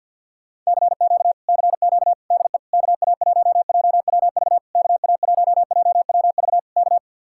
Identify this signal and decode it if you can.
Morse code, heard as a keyed tone, about 64 wpm, 700 Hz. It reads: CQ CQ DE KA1JWV KA1JWV K